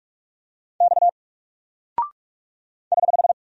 Morse code, heard as a keyed tone, about 44 wpm, 700 Hz. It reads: X <HH>
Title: Single Character (Letter:Number:Punctuation:Prosign) 45wpm